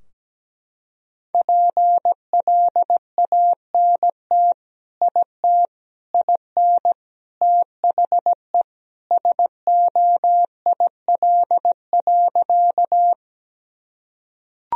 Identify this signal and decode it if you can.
Morse code, heard as a keyed tone, about 17 wpm, 700 Hz. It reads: PLANT IT IN THE SOIL.
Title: Sentences from Top 1000 Words - Review Part 4 - 17wpm